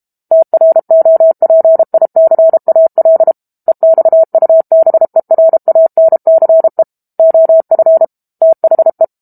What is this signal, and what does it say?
Morse code, heard as a keyed tone, about 32 wpm, 650 Hz. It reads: TROPICAL EXUBERANCE OF THE
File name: Book Two / 06-Chapter 6 - 32 WPM